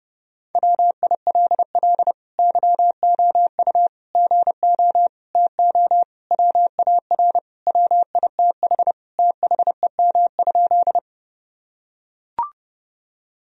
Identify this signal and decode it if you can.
Morse code, heard as a keyed tone, about 30 wpm, 700 Hz. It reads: WILL YOU GO TO WAR WITH THEM?
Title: Sentences from Top 1000 Words - Review Part 2 - 30wpm